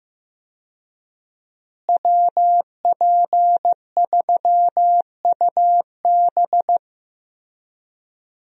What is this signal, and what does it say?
Morse code, heard as a keyed tone, about 15 wpm, 700 Hz. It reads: WP3UB
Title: USA Call Signs - 5 Characters - Repeated 2x - 15wpm